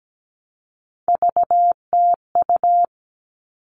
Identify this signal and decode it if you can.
Morse code, heard as a keyed tone, about 17 wpm, 700 Hz. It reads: VTU
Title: All Cut Numbers - 3-Digit 17wpm